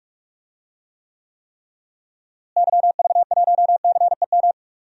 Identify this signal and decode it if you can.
Morse code, heard as a keyed tone, about 46 wpm, 700 Hz. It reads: YV1CEM
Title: DXCC-100 Call Signs - 6 Characters - Repeated 3x - 45wpm